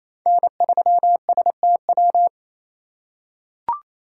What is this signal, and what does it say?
Morse code, heard as a keyed tone, about 28 wpm, 690 Hz. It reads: N3STW
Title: USA Call Signs - 5 Characters - 4x Character Spacing - 28wpm